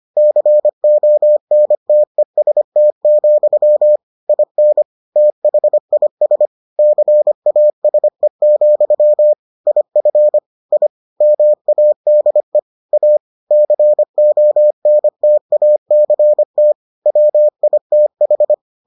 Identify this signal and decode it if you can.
Morse code, heard as a keyed tone, about 25 wpm, 600 Hz. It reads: CONTEST, IN THIS CASE, IF I MADE A CONTACT WITH